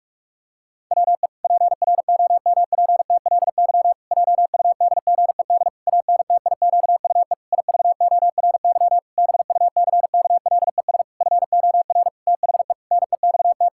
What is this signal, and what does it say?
Morse code, heard as a keyed tone, about 41 wpm, 700 Hz. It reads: WE PROMPTLY JUDGED ANTIQUE IVORY BUCKLES FOR THE NEXT
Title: Pangram Challenge v2 - 45wpm